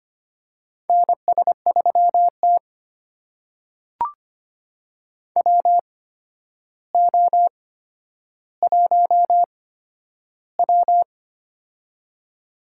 Morse code, heard as a keyed tone, about 25 wpm, spaced as for 11 wpm, 700 Hz. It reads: NS3T WO1W
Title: USA Call Signs - 4 Characters - 8x Character Spacing - 25wpm